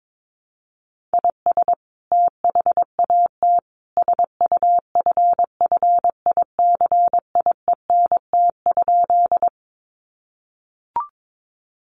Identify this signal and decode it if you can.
Morse code, heard as a keyed tone, about 22 wpm, 700 Hz. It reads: IS THAT SUFFICIENT?
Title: Sentences from Top 1400 Words 22wpm